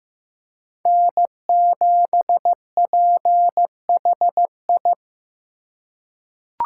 Morse code, heard as a keyed tone, about 15 wpm, 700 Hz. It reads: N7PHI
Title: K1USN SST QSO Practice - 15wpm